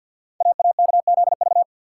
Morse code, heard as a keyed tone, about 53 wpm, 700 Hz. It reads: AAKZV
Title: Code Groups - 5 Letter:Numbers - 50wpm